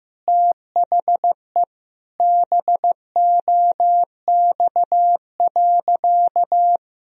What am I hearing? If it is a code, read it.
Morse code, heard as a keyed tone, about 15 wpm, 700 Hz. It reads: THE BOX.
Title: Sentences from Top 1000 Words - Review Part 2 - 15wpm